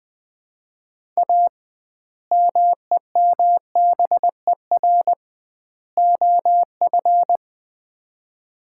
Morse code, heard as a keyed tone, about 20 wpm, 700 Hz. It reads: A MEMBER OF
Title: 2x Word-Spacing - Sets of 3 Words - 20wpm